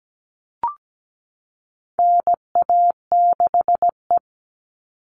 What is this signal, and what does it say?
Morse code, heard as a keyed tone, about 17 wpm, 700 Hz. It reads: NA6E
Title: All Cut Numbers - 4-Digit 17wpm